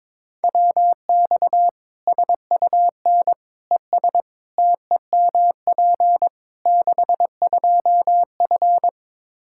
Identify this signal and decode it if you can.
Morse code, heard as a keyed tone, about 22 wpm, 700 Hz. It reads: WX SUN ES TEMP 62F
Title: Practice QSO - Exchange 2 - Info - 22wpm